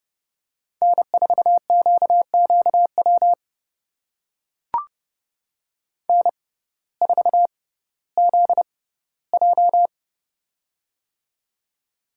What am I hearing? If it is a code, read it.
Morse code, heard as a keyed tone, about 32 wpm, spaced as for 17 wpm, 700 Hz. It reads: N4QQW N4ZJ
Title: USA Call Signs - 6x Character Spacing - 30wpm